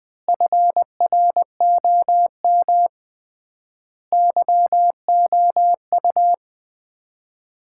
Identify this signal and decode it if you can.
Morse code, heard as a keyed tone, about 20 wpm, 700 Hz. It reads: FROM YOU
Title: Common QSO Phrases - 3x Word-Spacing - 20wpm